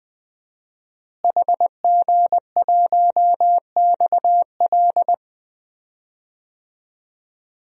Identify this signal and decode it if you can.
Morse code, heard as a keyed tone, about 20 wpm, 700 Hz. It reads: HG1XL
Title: DXCC-100 Call Signs - 5 Characters - Repeated 2x - 20wpm